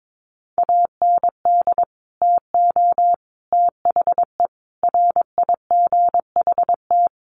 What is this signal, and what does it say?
Morse code, heard as a keyed tone, about 22 wpm, 700 Hz. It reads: AND TO THE RIGHT
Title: Sentences from Top 1300 Words 22wpm